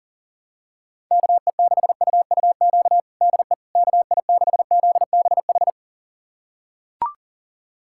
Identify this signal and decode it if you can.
Morse code, heard as a keyed tone, about 40 wpm, 700 Hz. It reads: KE6UUQ DE KI6ZBH
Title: USA Callsign DE Callsign 40wpm